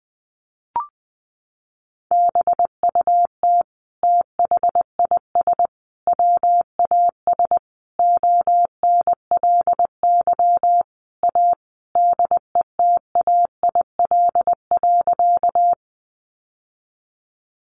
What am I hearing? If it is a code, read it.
Morse code, heard as a keyed tone, about 20 wpm, 700 Hz. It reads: BUT THIS WAS ONLY A DETAIL.
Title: Sentences from Top 1500 Words 20wpm